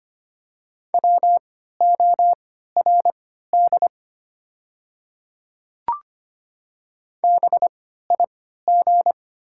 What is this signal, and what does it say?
Morse code, heard as a keyed tone, about 25 wpm, spaced as for 14 wpm, 700 Hz. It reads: WORD BIG
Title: 3x Character Spacing - Top 500 Words - 25wpm